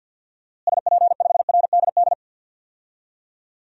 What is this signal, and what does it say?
Morse code, heard as a keyed tone, about 47 wpm, 700 Hz. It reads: SP5RDD
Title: DXCC-100 Call Signs - 6 Characters - Repeated 2x - 50wpm